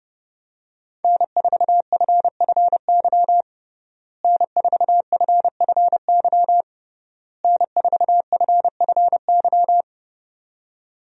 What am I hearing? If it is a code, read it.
Morse code, heard as a keyed tone, about 30 wpm, 700 Hz. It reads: N4FFY N4FFY N4FFY
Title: USA Call Signs - 5 Characters - Repeated 3x - 30wpm